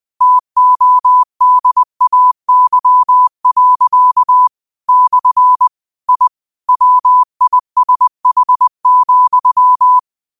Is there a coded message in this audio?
Yes: Morse code from 0.2 to 10.0 s, heard as a keyed tone, about 20 wpm, 1000 Hz. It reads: TODAY. / I WISH,